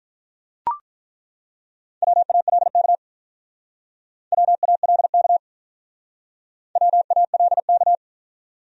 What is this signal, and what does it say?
Morse code, heard as a keyed tone, about 43 wpm, 700 Hz. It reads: WALK WALK WALK
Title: Speed-Racing Top 300 Words 35wpm